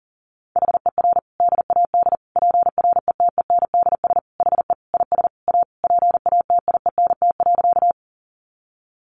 Morse code, heard as a keyed tone, about 37 wpm, 700 Hz. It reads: HER DAD PRETENDS HE IS A PATIENT.
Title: Additional Easy Phrases 40wpm